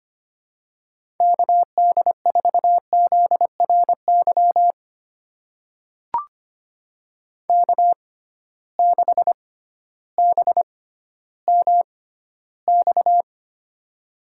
Morse code, heard as a keyed tone, about 25 wpm, spaced as for 13 wpm, 700 Hz. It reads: KD4ZRY K6BMX
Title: USA Call Signs - 6x Character Spacing - 25wpm